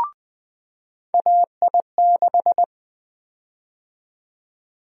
Morse code, heard as a keyed tone, about 20 wpm, 700 Hz. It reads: AI6
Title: USA Call Sign Prefixes - 20wpm